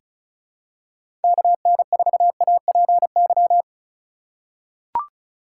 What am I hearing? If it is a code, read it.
Morse code, heard as a keyed tone, about 34 wpm, 700 Hz. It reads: KN4APY
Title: USA Call Signs - 2x Character Spacing - 35wpm